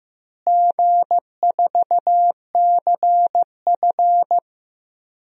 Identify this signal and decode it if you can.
Morse code, heard as a keyed tone, about 15 wpm, 700 Hz. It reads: G4CF
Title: DXCC-100 Call Signs - 4 Characters - Repeated 3x - 15wpm